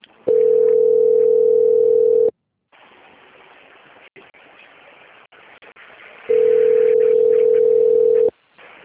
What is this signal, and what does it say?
Ringback tone (North America), 440 + 480 Hz.
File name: out- - - - - .20768